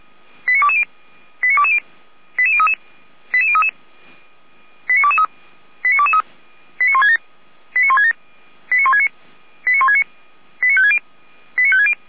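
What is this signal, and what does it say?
Five-tone selective call (ZVEI), heard as quick runs of five tones, tones of about 70 ms. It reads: 89220 89220 88033 88033 89223 89223 89167 89169 89169 89560 89560
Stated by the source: ZVEI five-tone selective call code 89163, received by SDR.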